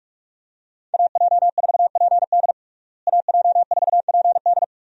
Morse code, heard as a keyed tone, about 46 wpm, 700 Hz. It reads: AJ4PD AJ4PD
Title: USA Call Signs - 5 Characters - Repeated 2x - 45wpm